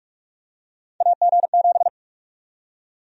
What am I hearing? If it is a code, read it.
Morse code, heard as a keyed tone, about 46 wpm, 700 Hz. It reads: AG7
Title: USA Call Sign Prefixes - 45wpm